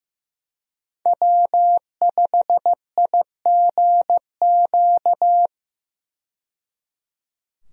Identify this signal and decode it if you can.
Morse code, heard as a keyed tone, about 15 wpm, 700 Hz. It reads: W5IGQ
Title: USA Call Signs - 5 Characters - 15wpm